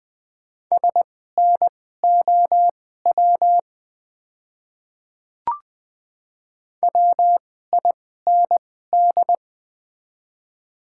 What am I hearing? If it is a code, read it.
Morse code, heard as a keyed tone, about 20 wpm, spaced as for 14 wpm, 700 Hz. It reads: SNOW WIND
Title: 2x Character Spacing - Top 500 Words - 20wpm